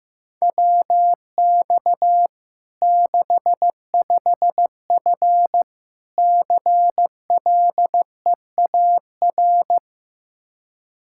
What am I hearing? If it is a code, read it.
Morse code, heard as a keyed tone, about 15 wpm, 700 Hz. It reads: WX 65F CLEAR
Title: Practice QSO - Exchange 2 - Info - 15wpm